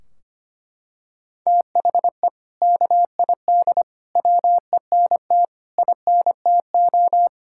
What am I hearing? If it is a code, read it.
Morse code, heard as a keyed tone, about 25 wpm, 700 Hz. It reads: THE KID WENT INTO
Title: Sentences from Top 1600 Words 25wpm